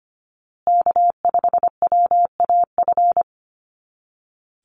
Morse code, heard as a keyed tone, about 25 wpm, 700 Hz. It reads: K5WAF
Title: Call Signs Repeated 2x - 25wpm